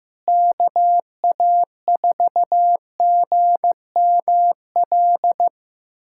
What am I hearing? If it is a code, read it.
Morse code, heard as a keyed tone, about 15 wpm, 700 Hz. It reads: KA4GML
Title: USA Callsign DE Callsign - 6 Characters - 15wpm